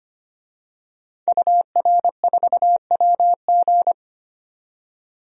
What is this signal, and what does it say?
Morse code, heard as a keyed tone, about 25 wpm, 700 Hz. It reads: UR4WG
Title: DXCC-100 Call Signs - 5 Characters - 25wpm